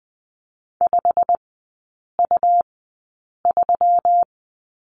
Morse code, heard as a keyed tone, about 20 wpm, spaced as for 8 wpm, 700 Hz. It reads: 5U3
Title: 121 Lesson 20e - Number 3 - Three Characters 20wpm